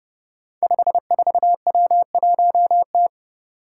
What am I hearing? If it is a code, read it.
Morse code, heard as a keyed tone, about 30 wpm, 700 Hz. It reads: 54W1T
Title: Five Challenging Letter-Number Combinations 30wpm